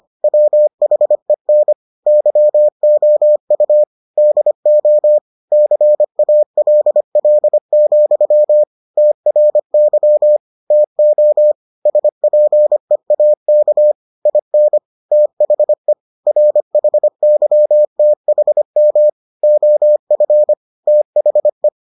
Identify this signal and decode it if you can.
Morse code, heard as a keyed tone, about 25 wpm, 600 Hz. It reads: WHEN YOU DO CALL, TRY TO SPEAK IN THE RHYTHM OF THE